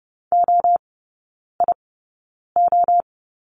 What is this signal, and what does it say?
Morse code, heard as a keyed tone, about 30 wpm, spaced as for 9 wpm, 700 Hz. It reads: OIO
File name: Lesson 18e - Letter m - Three Characters 30wpm